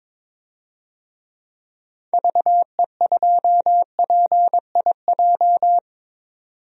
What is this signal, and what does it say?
Morse code, heard as a keyed tone, about 21 wpm, 700 Hz. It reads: VE2PIJ
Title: DXCC-100 Call Signs - 6 Characters - 22wpm